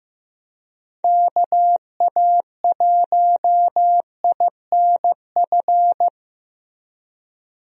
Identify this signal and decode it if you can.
Morse code, heard as a keyed tone, about 15 wpm, 700 Hz. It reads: KA1INF
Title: USA Call Signs - 6 Characters - Repeated 3x - 15wpm